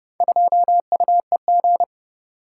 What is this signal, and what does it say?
Morse code, heard as a keyed tone, about 28 wpm, 700 Hz. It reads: 2UEG